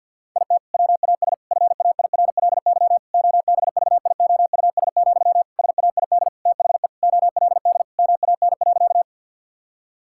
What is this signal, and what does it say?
Morse code, heard as a keyed tone, about 50 wpm, 700 Hz. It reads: IT WAS FAIRLY OBVIOUS, SAID THE OLD MAN.
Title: Sentences from Top 1600 Words 50wpm